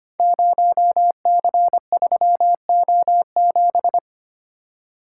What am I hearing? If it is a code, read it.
Morse code, heard as a keyed tone, about 25 wpm, 700 Hz. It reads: 0C3O7
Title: Code Groups - 5 Letter:Numbers - 25wpm